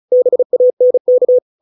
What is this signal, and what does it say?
Morse code, heard as a keyed tone, about 34 wpm, 500 Hz. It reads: BANK
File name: 632 words / 632 Words 35WPM